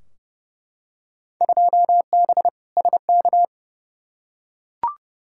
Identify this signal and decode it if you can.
Morse code, heard as a keyed tone, about 29 wpm, 700 Hz. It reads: 2B SK
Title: Field Day Exchanges Encore - 30wpm